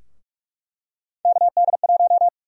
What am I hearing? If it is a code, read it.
Morse code, heard as a keyed tone, about 44 wpm, 700 Hz. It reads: KD1
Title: USA Call Sign Prefixes - Repeated 2x - 45wpm